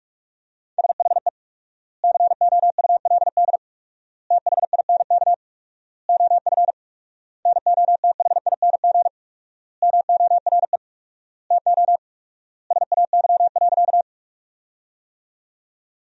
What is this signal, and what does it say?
Morse code, heard as a keyed tone, about 43 wpm, 700 Hz. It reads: SHE COULD THINK OF NOTHING MORE TO SAY.
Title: 4x Word-Spacing - Sentences from Top 500 Words - 45wpm